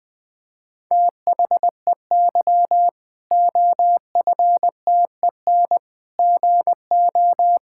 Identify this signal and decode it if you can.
Morse code, heard as a keyed tone, about 20 wpm, 700 Hz. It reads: THEY OFTEN GO
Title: Difficult Sentences with Hints - 20wpm